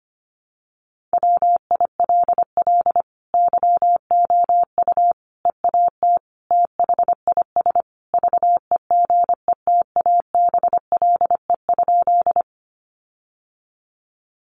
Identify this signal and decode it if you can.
Morse code, heard as a keyed tone, about 25 wpm, 700 Hz. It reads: WILL YOU EAT THIS VEGETABLE?
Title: Sentences from Top 1300 Words 25wpm